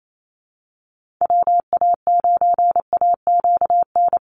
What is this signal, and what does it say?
Morse code, heard as a keyed tone, about 28 wpm, 700 Hz. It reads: WA9AQN
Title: Call Signs Repeated 2x - 28wpm